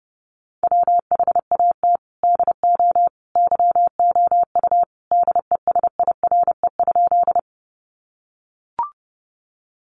Morse code, heard as a keyed tone, about 30 wpm, 700 Hz. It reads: WHAT DO YOU DESIRE?
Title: Sentences from Top 1100 Words 30wpm